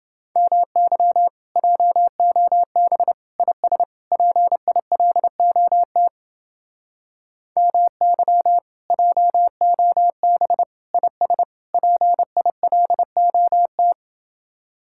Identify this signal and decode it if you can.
Morse code, heard as a keyed tone, about 30 wpm, 700 Hz. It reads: MY JOB IS PILOT MY JOB IS PILOT